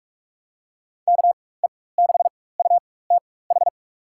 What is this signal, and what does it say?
Morse code, heard as a keyed tone, about 46 wpm, spaced as for 21 wpm, 700 Hz. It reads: KE6UTH
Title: USA Call Signs - 6 Characters - 4x Character Spacing - 45wpm